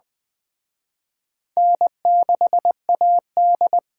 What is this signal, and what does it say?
Morse code, heard as a keyed tone, about 21 wpm, 700 Hz. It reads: N6AD
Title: Speed Racing - USA 1x2 Call Signs 17wpm